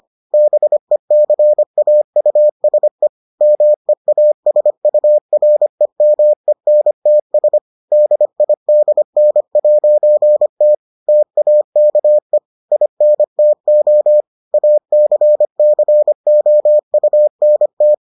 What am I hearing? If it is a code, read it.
Morse code, heard as a keyed tone, about 25 wpm, 600 Hz. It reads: BECAUSE MEASUREMENTS DIDN'T TAKE INTO ACCOUNT